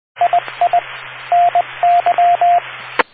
Morse code, heard as a keyed tone, about 20 wpm, 700 Hz. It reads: II NY